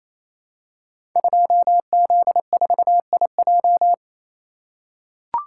Morse code, heard as a keyed tone, about 28 wpm, 700 Hz. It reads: 2Z4IJ
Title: Code Groups - 5 Letter:Numbers - 28wpm